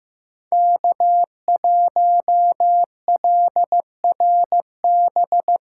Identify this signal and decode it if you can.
Morse code, heard as a keyed tone, about 15 wpm, 700 Hz. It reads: K1LRB
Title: USA Call Signs - 5 Characters - 3x Character Spacing - 15wpm